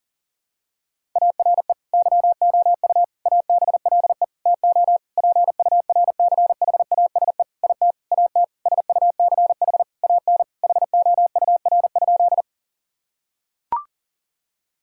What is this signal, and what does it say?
Morse code, heard as a keyed tone, about 40 wpm, 700 Hz. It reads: ARE YOU ABLE TO PURCHASE IT AT SUCH AN HOUR?
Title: Sentences from Top 1000 Words - Review Part 5 - 40wpm